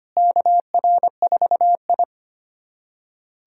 Morse code, heard as a keyed tone, about 25 wpm, 700 Hz. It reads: KR4I
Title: USA Callsign DE Callsign 25wpm